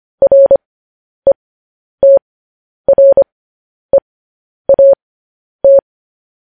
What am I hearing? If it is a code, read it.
Morse code, heard as a keyed tone, about 25 wpm, spaced as for 10 wpm, 550 Hz. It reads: RETREAT
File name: podcast / News headlines-1x-25x10